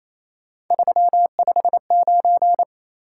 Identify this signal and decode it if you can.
Morse code, heard as a keyed tone, about 28 wpm, 700 Hz. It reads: 359
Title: Top 180 QSO Elements 28wpm